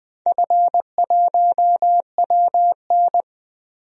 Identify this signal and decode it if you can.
Morse code, heard as a keyed tone, about 20 wpm, 700 Hz. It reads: F1WN
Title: DXCC-100 Call Signs - 4 Characters - 20wpm